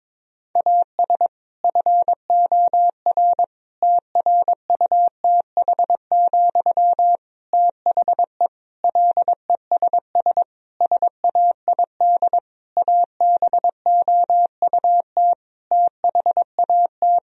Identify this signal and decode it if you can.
Morse code, heard as a keyed tone, about 22 wpm, 700 Hz. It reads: AS FOR TRUTH, THE LESS SAID ABOUT THAT
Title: Sentences from Top 1000 Words - Review Part 2 - 22wpm